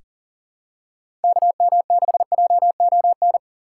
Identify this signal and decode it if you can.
Morse code, heard as a keyed tone, about 37 wpm, 700 Hz. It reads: KM6JON
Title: USA Call Signs - 6 Characters - 6x Character Spacing - 40wpm